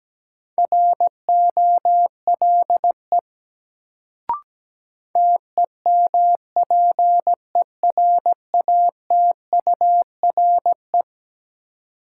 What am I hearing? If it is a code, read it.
Morse code, heard as a keyed tone, about 17 wpm, 700 Hz. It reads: ROLE TEMPERATURE
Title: Sentences from Top 1500 Words - New Word Drill Alt - 17wpm